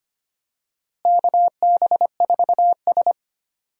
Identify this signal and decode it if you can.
Morse code, heard as a keyed tone, about 25 wpm, 700 Hz. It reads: KB4S
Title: USA Call Signs - 4 Characters - Repeated 2x - 25wpm